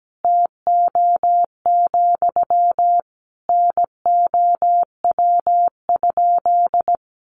Morse code, heard as a keyed tone, about 17 wpm, 700 Hz. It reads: TO, NOW?
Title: Sentences from Top 200 Words 17wpm